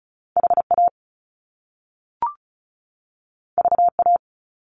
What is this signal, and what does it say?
Morse code, heard as a keyed tone, about 36 wpm, 700 Hz. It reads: HA VA
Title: Top Two Letter Combinations in Words 35wpm